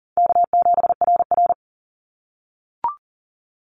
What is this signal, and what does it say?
Morse code, heard as a keyed tone, about 40 wpm, 700 Hz. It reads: K7RR
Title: Call Signs - 40wpm